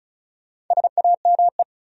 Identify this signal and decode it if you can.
Morse code, heard as a keyed tone, about 48 wpm, spaced as for 41 wpm, 700 Hz. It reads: SAME
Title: Speed-Racing Top 200 Words 35wpm